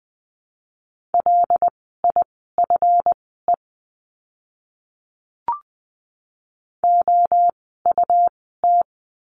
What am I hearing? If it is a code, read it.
Morse code, heard as a keyed tone, about 20 wpm, spaced as for 14 wpm, 700 Hz. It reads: LIFE OUT
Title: 206 Lesson 33f - Letter j - Words 20wpm